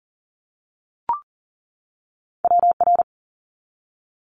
Morse code, heard as a keyed tone, about 41 wpm, 700 Hz. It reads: WR
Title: Multi-letter Phonemes 40wpm